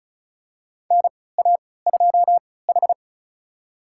Morse code, heard as a keyed tone, about 36 wpm, spaced as for 20 wpm, 700 Hz. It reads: NA2H
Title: USA Call Signs - 4 Characters - 3x Character Spacing - 35wpm